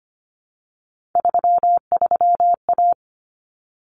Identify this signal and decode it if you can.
Morse code, heard as a keyed tone, about 25 wpm, 700 Hz. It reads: 33A
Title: Common Cut Numbers - 3-Digit 25wpm